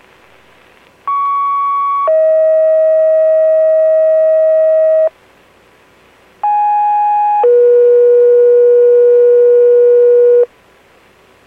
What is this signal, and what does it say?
Two-tone page, heard as a one-second tone, then a three-second tone, about 1120 then 640 Hz, about 830 then 480 Hz. It is heard from 1.1 to 10.4 s.